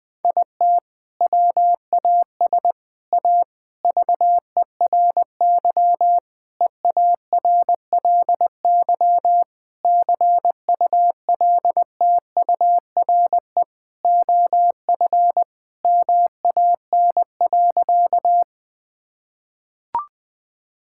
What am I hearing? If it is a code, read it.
Morse code, heard as a keyed tone, about 20 wpm, 700 Hz. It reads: IT WAS A VERY EARLY CULTURE OF MAN.
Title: Sentences from Top 1000 Words - Review Part 6 - 20wpm